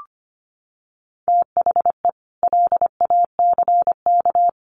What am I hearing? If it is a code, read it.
Morse code, heard as a keyed tone, about 25 wpm, 700 Hz. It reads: THE LACK
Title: Sets of 3 Words 25wpm